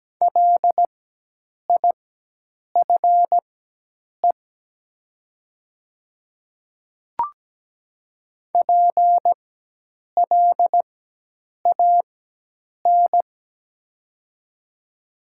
Morse code, heard as a keyed tone, about 17 wpm, spaced as for 8 wpm, 700 Hz. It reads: LIFE PLAN
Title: 4x Character Spacing - Top 500 Words - 17wpm